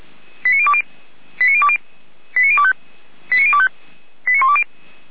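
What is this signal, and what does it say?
Five-tone selective call (ZVEI), heard as quick runs of five tones, tones of about 70 ms. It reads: 89020 89020 89025 89025 89120
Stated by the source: Five-tone ZVEI selective call received on an SDR.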